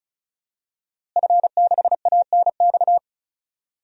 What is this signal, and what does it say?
Morse code, heard as a keyed tone, about 36 wpm, 700 Hz. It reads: F6ANX
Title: DXCC-100 Call Signs - 5 Characters - Repeated 2x - 35wpm